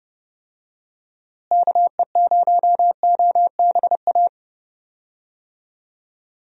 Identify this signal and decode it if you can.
Morse code, heard as a keyed tone, about 31 wpm, 700 Hz. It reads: KE0OBA
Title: USA Call Signs - 6 Characters - Repeated 2x - 30wpm